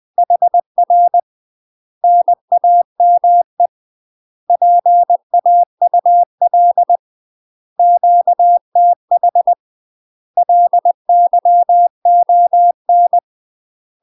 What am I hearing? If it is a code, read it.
Morse code, heard as a keyed tone, about 20 wpm, 700 Hz. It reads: HR NAME PAUL QTH LYON
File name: UK MASTER QSOs TRAINING / 20WPM / UK MASTER QSOs 0063